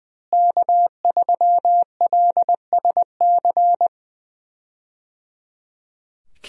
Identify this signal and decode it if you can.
Morse code, heard as a keyed tone, about 21 wpm, 700 Hz. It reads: K3LSC
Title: USA Call Signs - 5 Characters - Repeated 3x - 20wpm